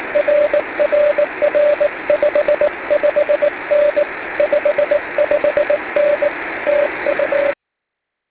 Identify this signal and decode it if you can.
Morse code, heard as a keyed tone, about 18 wpm, 590 Hz, with radio noise. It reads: RRR 55N 55N TU